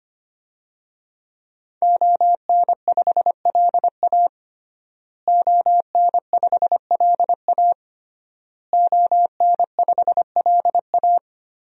Morse code, heard as a keyed tone, about 25 wpm, 700 Hz. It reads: ON5LA ON5LA ON5LA
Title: DXCC-100 Call Signs - 5 Characters - Repeated 3x - 25wpm